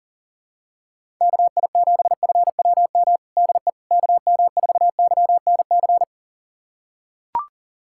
Morse code, heard as a keyed tone, about 40 wpm, 700 Hz. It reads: KI7FWM DE KM4YNC
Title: USA Callsign DE Callsign - 6 Characters - 40wpm